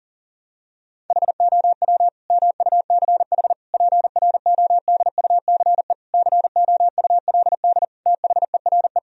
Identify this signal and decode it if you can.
Morse code, heard as a keyed tone, about 40 wpm, 700 Hz. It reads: HOW MUCH PRODUCE COULD THERE
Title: Sentences from Top 1000 Words - Review Part 3 - 40wpm